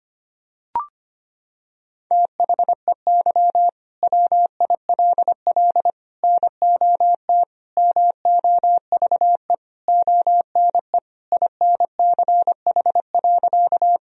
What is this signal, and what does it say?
Morse code, heard as a keyed tone, about 25 wpm, 700 Hz. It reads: THEY WILL NOT MOVE ONE INCH.
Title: Sentences from Top 1000 Words - Review Part 3 - 25wpm